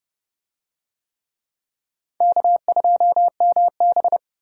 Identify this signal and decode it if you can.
Morse code, heard as a keyed tone, about 27 wpm, 700 Hz. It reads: K2MB